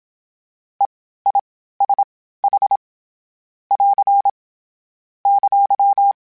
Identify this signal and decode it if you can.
Morse code, heard as a keyed tone, about 27 wpm, spaced as for 15 wpm, 800 Hz. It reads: EISH <AR> !
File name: Short Runs / Short Runs-mp3s / All DITs @25